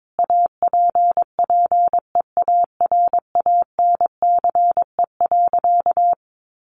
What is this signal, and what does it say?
Morse code, heard as a keyed tone, about 22 wpm, 700 Hz. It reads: APPEARANCE.
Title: Sentences from Top 1100 Words 22wpm